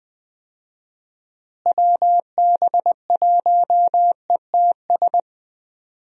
Morse code, heard as a keyed tone, about 20 wpm, 700 Hz. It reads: WB1ETS
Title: USA Call Signs - 6 Characters - Repeated 2x - 20wpm